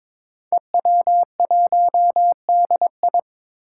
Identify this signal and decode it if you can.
Morse code, heard as a keyed tone, about 22 wpm, 700 Hz. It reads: EW1DI
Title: Five Challenging Letter-Number Combinations 22wpm